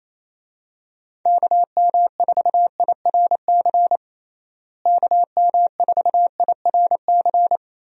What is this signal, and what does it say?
Morse code, heard as a keyed tone, about 28 wpm, 700 Hz. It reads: KM4IRC KM4IRC
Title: USA Call Signs - 6 Characters - Repeated 3x - 28wpm